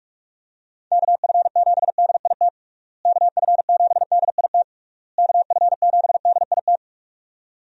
Morse code, heard as a keyed tone, about 41 wpm, 700 Hz. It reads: KF7DIT KF7DIT KF7DIT
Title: USA Call Signs - 6 Characters - Repeated 3x - 45wpm